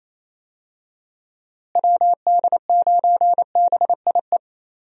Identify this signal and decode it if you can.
Morse code, heard as a keyed tone, about 27 wpm, 700 Hz. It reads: WD9BIE
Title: USA Call Signs - 6 Characters - Repeated 3x - 28wpm